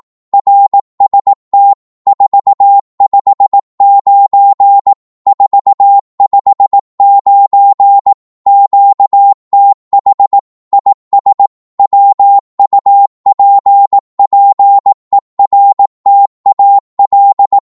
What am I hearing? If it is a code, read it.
Morse code, heard as a keyed tone, about 18 wpm, 800 Hz. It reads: RST 459 459 QTH IS WUPPERTAL
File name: Morse / qso-18wpm